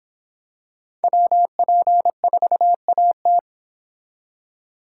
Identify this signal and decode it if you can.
Morse code, heard as a keyed tone, about 26 wpm, 700 Hz. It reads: WP4AT